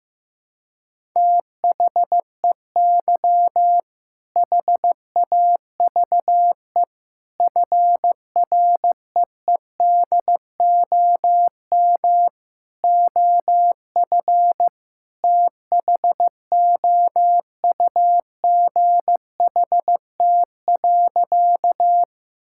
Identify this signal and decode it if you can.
Morse code, heard as a keyed tone, about 16 wpm, 700 Hz. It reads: THEY HAVE FREEDOM OF THOUGHT.
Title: Sentences from Top 1000 Words - Review Part 5 - 15wpm